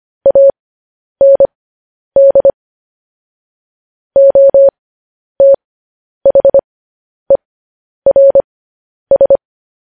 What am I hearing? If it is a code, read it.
Morse code, heard as a keyed tone, about 25 wpm, spaced as for 10 wpm, 550 Hz. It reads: AND OTHERS
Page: news headlines in morse code at 10 wpm